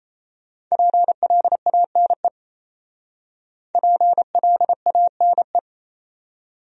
Morse code, heard as a keyed tone, about 36 wpm, 700 Hz. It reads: PLANE PLANE